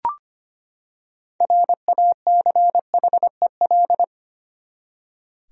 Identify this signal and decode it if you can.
Morse code, heard as a keyed tone, about 25 wpm, 700 Hz. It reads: RACHEL
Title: USA Common Names - 25wpm